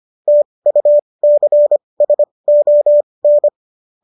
Morse code, heard as a keyed tone, about 25 wpm, spaced as for 20 wpm, 590 Hz. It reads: TUCSON